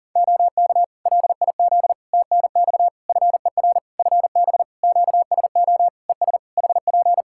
Morse code, heard as a keyed tone, about 40 wpm, 700 Hz. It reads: OK LIZ TNX FER FB QSO ES HP
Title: K1USN SST QSO Practice 40wpm